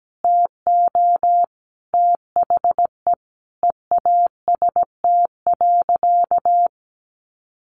Morse code, heard as a keyed tone, about 17 wpm, 700 Hz. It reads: TO THE EAST.
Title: Sentences from Top 1300 Words 17wpm